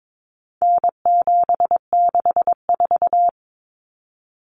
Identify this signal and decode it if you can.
Morse code, heard as a keyed tone, about 22 wpm, 700 Hz. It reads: N764
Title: Common Cut Numbers - 4-Digit 22wpm